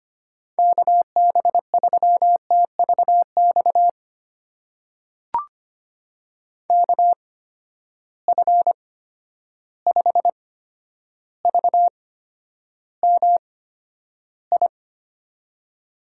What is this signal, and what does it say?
Morse code, heard as a keyed tone, about 25 wpm, spaced as for 11 wpm, 700 Hz. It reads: KB3TVX KF5VMI
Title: USA Call Signs - 6 Characters - 8x Character Spacing - 25wpm